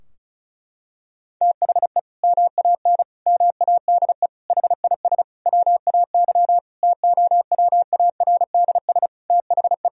Morse code, heard as a keyed tone, about 36 wpm, 700 Hz. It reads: THE MAN MADE HIS WAY TOWARDS THE
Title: Sentences from Top 1000 Words - Review Part 2 - 35wpm